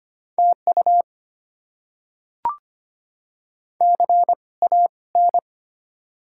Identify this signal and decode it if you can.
Morse code, heard as a keyed tone, about 25 wpm, spaced as for 20 wpm, 700 Hz. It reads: TU CAN
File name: Top 90 QSO Elements - 2x Character Spacing - 25wpm